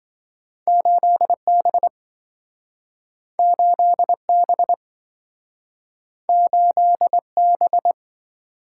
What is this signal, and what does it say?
Morse code, heard as a keyed tone, about 27 wpm, 700 Hz. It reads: 8B 8B 8B